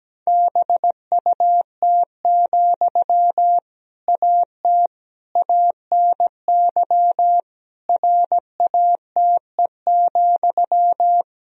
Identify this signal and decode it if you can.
Morse code, heard as a keyed tone, about 17 wpm, 700 Hz. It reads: BUT, AT ANY RATE,
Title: Sentences from Top 1000 Words - Review Part 4 - 17wpm